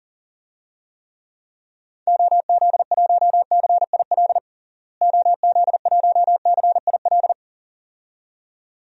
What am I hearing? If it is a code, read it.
Morse code, heard as a keyed tone, about 40 wpm, 700 Hz. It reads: OZ1CIL OZ1CIL
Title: DXCC-100 Call Signs - 6 Characters - Repeated 2x - 40wpm